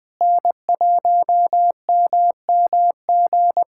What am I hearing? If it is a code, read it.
Morse code, heard as a keyed tone, about 20 wpm, 700 Hz. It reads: N1MMG
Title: USA Callsign DE Callsign - 5 Characters - 20wpm